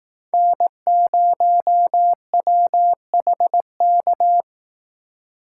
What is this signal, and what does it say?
Morse code, heard as a keyed tone, about 18 wpm, 700 Hz. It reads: N0WHK
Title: Speed Racing - USA 1x3 Call Signs 15wpm